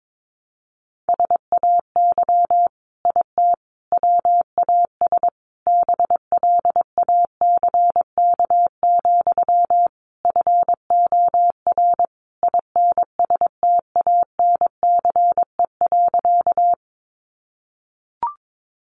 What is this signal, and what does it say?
Morse code, heard as a keyed tone, about 22 wpm, 700 Hz. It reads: SAY IT WAS BLACK, FOR INSTANCE.
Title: Sentences from Top 800 Words 22wpm